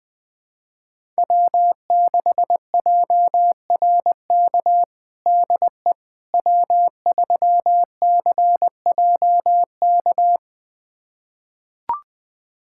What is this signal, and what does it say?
Morse code, heard as a keyed tone, about 20 wpm, 700 Hz. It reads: W6JRK DE W3CJK
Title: USA Callsign DE Callsign - 5 Characters - 20wpm